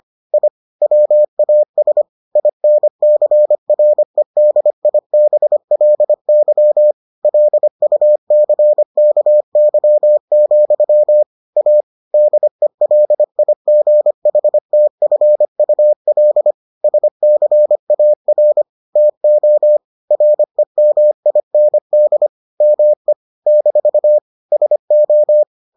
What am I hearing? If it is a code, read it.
Morse code, heard as a keyed tone, about 25 wpm, 600 Hz. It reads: I WAS INCREDIBLY LUCKY, A DELIGHTFUL SCAR TO REMIND ME - SO